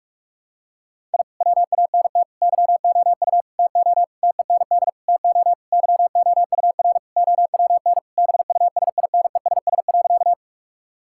Morse code, heard as a keyed tone, about 46 wpm, 700 Hz. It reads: I WANT YOU TO TEND TO YOUR OWN BUSINESS.
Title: Sentences from Top 1000 Words - Review Part 5 - 45wpm